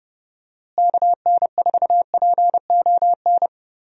Morse code, heard as a keyed tone, about 30 wpm, 700 Hz. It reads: KN4PON
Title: USA Call Signs - 6 Characters - Repeated 3x - 30wpm